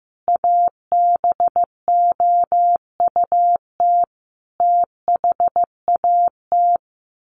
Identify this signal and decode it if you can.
Morse code, heard as a keyed tone, about 15 wpm, 700 Hz. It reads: ABOUT THAT
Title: Sets of 5 Words Encore2 15wpm